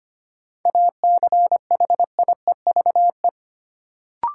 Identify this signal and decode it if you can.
Morse code, heard as a keyed tone, about 25 wpm, 700 Hz. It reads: ACHIEVE
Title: Sentences from Top 1000 Words - Review Part 5 - New Word Drill - 25wpm